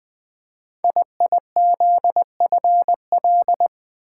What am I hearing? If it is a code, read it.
Morse code, heard as a keyed tone, about 20 wpm, 700 Hz. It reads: IIZFL
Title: Code Groups - 5 Letters - 20wpm